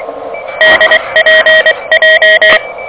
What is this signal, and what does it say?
Morse code, heard as a keyed tone, about 23 wpm, 1900 Hz, with noise. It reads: DPJ
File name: OK QRP 2023 / OK QRP 2023 OK1DWF